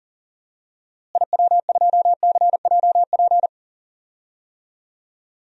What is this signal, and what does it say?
Morse code, heard as a keyed tone, about 38 wpm, 700 Hz. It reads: IW2CJP
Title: DXCC-100 Call Signs - 6 Characters - Repeated 3x - 40wpm